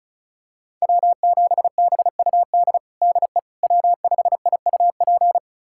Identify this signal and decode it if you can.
Morse code, heard as a keyed tone, about 34 wpm, 700 Hz. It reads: W7BUD DE W5IUP
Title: USA Callsign DE Callsign - 5 Characters - 35wpm